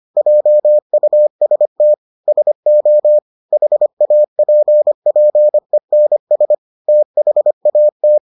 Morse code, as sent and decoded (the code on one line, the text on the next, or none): .--- ..- ... - / ... --- / .... .- .--. .--. -. ... / - .... .- -
JUST SO HAPPENS THAT